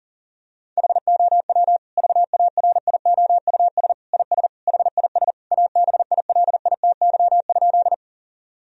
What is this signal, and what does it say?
Morse code, heard as a keyed tone, about 37 wpm, 700 Hz. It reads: HOW VARIOUS IS HIS ABILITY?
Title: Sentences from Top 1000 Words - Review Part 4 - 40wpm